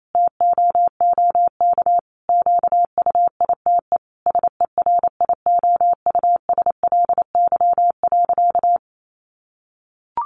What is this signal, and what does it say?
Morse code, heard as a keyed tone, about 28 wpm, 700 Hz. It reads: TOOK QUITE SERIOUSLY.
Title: Sentences from Top 1200 Words 28wpm